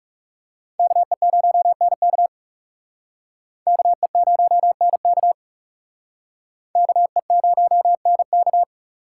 Morse code, heard as a keyed tone, about 38 wpm, 700 Hz. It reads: KE0NK KE0NK KE0NK